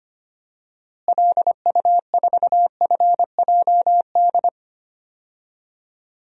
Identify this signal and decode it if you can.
Morse code, heard as a keyed tone, about 25 wpm, 700 Hz. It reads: LU4FJD
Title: DXCC-100 Call Signs - 6 Characters - Repeated 3x - 25wpm